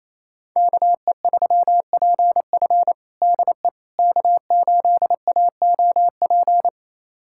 Morse code, heard as a keyed tone, about 28 wpm, 700 Hz. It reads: KE3PF DE K8AOP
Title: USA Callsign DE Callsign - 5 Characters - 28wpm